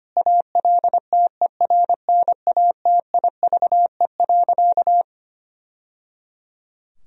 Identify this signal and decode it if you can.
Morse code, heard as a keyed tone, about 25 wpm, 700 Hz. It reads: ALTERNATIVE.
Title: Sentences from Top 1000 Words - Review Part 4 - 25wpm